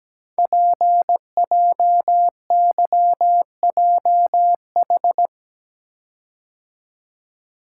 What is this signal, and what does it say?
Morse code, heard as a keyed tone, about 17 wpm, 700 Hz. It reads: PJYJH
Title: Code Groups - 5 Letters - 17wpm